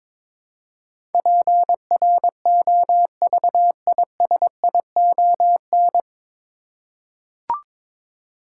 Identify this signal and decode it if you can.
Morse code, heard as a keyed tone, about 22 wpm, 700 Hz. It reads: PROVISION
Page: Morse Code Ninja practice 22wpm